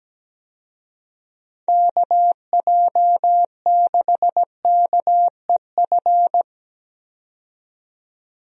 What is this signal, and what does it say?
Morse code, heard as a keyed tone, about 17 wpm, 700 Hz. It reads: KJ6KEF
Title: USA Call Signs - 6 Characters - 17wpm